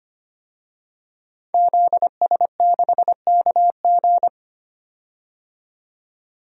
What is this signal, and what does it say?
Morse code, heard as a keyed tone, about 25 wpm, 700 Hz. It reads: ZS6KG
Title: DXCC-100 Call Signs - 5 Characters - 25wpm